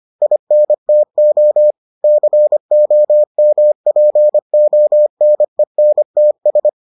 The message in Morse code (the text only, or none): INTO COMPONENTS